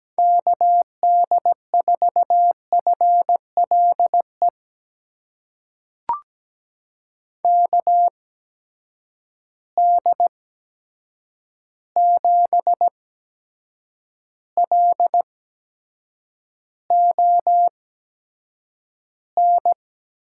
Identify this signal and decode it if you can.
Morse code, heard as a keyed tone, about 17 wpm, spaced as for 7 wpm, 700 Hz. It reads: KD4FLE KD7LON
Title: USA Call Signs - 6 Characters - 8x Character Spacing - 17wpm